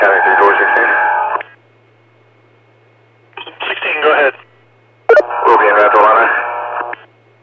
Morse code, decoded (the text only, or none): none